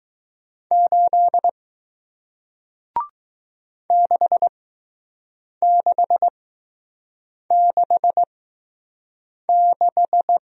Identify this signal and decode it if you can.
Morse code, heard as a keyed tone, about 20 wpm, spaced as for 6 wpm, 700 Hz. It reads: 8 6666